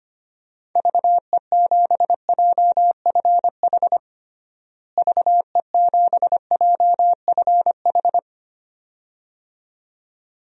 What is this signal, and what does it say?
Morse code, heard as a keyed tone, about 25 wpm, 700 Hz. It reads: VE7JFH VE7JFH
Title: DXCC-100 Call Signs - 6 Characters - Repeated 2x - 25wpm